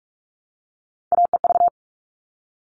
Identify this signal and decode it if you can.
Morse code, heard as a keyed tone, about 44 wpm, 700 Hz. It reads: AEV
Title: All Cut Numbers - 3-Digit 45wpm